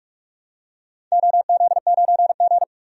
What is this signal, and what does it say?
Morse code, heard as a keyed tone, about 46 wpm, 700 Hz. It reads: OZ9G
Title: DXCC-100 Call Signs - 4 Characters - 45wpm